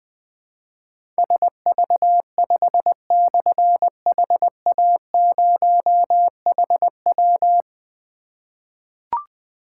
Morse code, heard as a keyed tone, about 20 wpm, 700 Hz. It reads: SV5/HA0HW
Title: Difficult Call Signs 20wpm